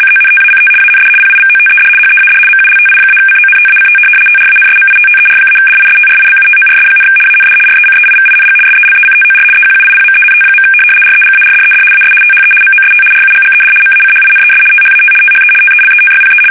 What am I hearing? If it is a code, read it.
RTTY, about 75 baud. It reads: CONTACT AND PIVOTED HEAD MULTI-CONTACT TRANSMITTER-DISTRIBUTORS, AND RECEIVING SELECTOR EQUIPMENT. REGARDED AS THE MOST RUGGED MACHINES THAT THE TELETYPE
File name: fsk850-75B / teletype-corp